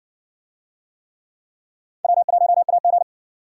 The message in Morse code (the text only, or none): W1AG